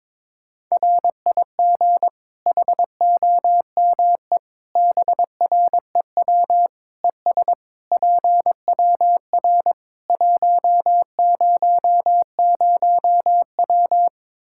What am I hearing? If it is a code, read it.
Morse code, heard as a keyed tone, about 22 wpm, 700 Hz. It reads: RIG HOME BREW ES PWR 100W
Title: Practice QSO - Exchange 2 - Info - 22wpm